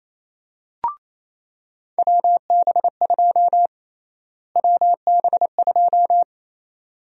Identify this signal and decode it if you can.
Morse code, heard as a keyed tone, about 27 wpm, 700 Hz. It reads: WB2 WB2
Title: USA Call Sign Prefixes - Repeated 2x - 28wpm